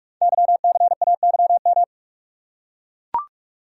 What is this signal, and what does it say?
Morse code, heard as a keyed tone, about 44 wpm, 700 Hz. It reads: YCAYM
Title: Code Groups - 5 Letters - 45wpm